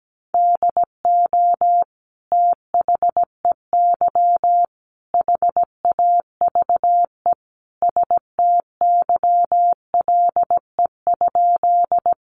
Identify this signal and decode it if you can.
Morse code, heard as a keyed tone, about 17 wpm, 700 Hz. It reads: DO THEY HAVE STYLE?
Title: Sentences from Top 800 Words 17wpm